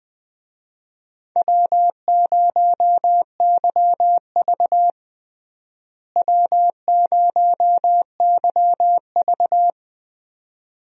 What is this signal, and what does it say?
Morse code, heard as a keyed tone, about 21 wpm, 700 Hz. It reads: W0YV W0YV
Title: USA Call Signs - 4 Characters - Repeated 2x - 20wpm